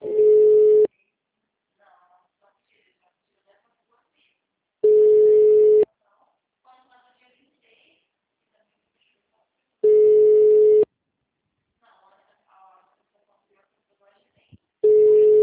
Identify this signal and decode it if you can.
Ringback tone (Europe, 425 Hz), with speech.
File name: out- -2012- - - .4996